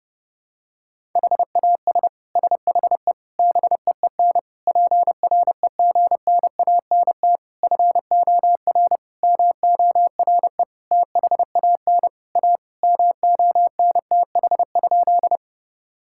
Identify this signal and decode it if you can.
Morse code, heard as a keyed tone, about 30 wpm, 700 Hz. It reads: HAS SHE BEEN PREGNANT FOR MORE THAN A MONTH?
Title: Sentences from Top 1000 Words - Review Part 6 - 30wpm